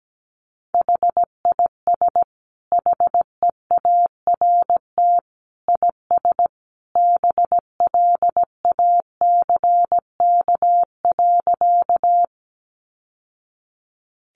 Morse code, heard as a keyed tone, about 17 wpm, 700 Hz. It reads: HIS HEART IS BLACK.
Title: Sentences from Top 1200 Words 17wpm